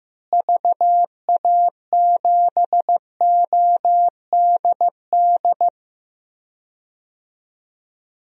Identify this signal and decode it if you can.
Morse code, heard as a keyed tone, about 15 wpm, 700 Hz. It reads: VA7ODD
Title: DXCC-100 Call Signs - 6 Characters - Repeated 3x - 15wpm